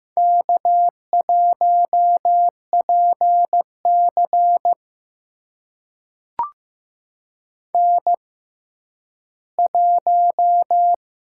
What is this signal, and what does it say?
Morse code, heard as a keyed tone, about 15 wpm, spaced as for 10 wpm, 700 Hz. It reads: K1PC N1
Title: USA Call Signs - 4 Characters - 6x Character Spacing - 15wpm